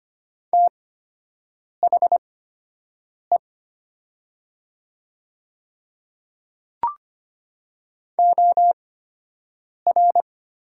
Morse code, heard as a keyed tone, about 25 wpm, spaced as for 7 wpm, 700 Hz. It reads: THE OR
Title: 8x Character Spacing - Top 100 Words - 25wpm